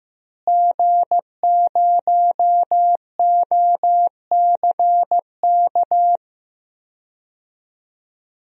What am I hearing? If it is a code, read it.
Morse code, heard as a keyed tone, about 15 wpm, 700 Hz. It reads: G0OCK
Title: DXCC-100 Call Signs - 5 Characters - Repeated 2x - 15wpm